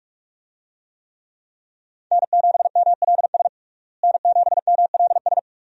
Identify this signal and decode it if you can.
Morse code, heard as a keyed tone, about 42 wpm, 700 Hz. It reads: N7MLS N7MLS
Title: USA Call Signs - 5 Characters - Repeated 2x - 45wpm